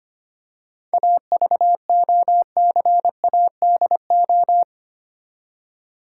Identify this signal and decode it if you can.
Morse code, heard as a keyed tone, about 25 wpm, 700 Hz. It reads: AVOCADO
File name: practice / 25wpm / Difficult Words with Hints - 25wpm